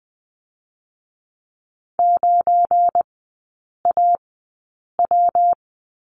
Morse code, heard as a keyed tone, about 20 wpm, spaced as for 8 wpm, 700 Hz. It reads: 9AW
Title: 172 Lesson 28e - Number 9 - Three Characters 20wpm